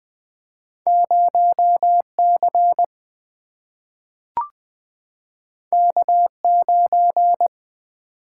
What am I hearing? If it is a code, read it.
Morse code, heard as a keyed tone, about 20 wpm, 700 Hz. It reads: 0C K9
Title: Single-Letter Single-Number 20wpm